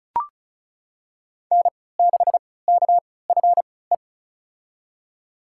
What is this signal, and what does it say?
Morse code, heard as a keyed tone, about 36 wpm, spaced as for 20 wpm, 700 Hz. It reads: N6KFE
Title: USA Call Signs - 5 Characters - 3x Character Spacing - 35wpm